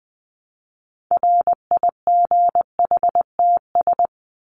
Morse code, heard as a keyed tone, about 20 wpm, 700 Hz. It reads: RIGHTS
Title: ES:S Suffix - Top 100 Words with Base Words 20wpm